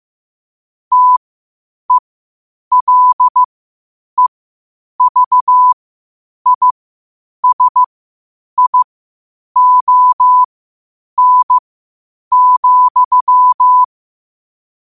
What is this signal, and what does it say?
Morse code, heard as a keyed tone, about 15 wpm, spaced as for 8 wpm, 1000 Hz. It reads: TELEVISION,
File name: qotdfiles / 8wpm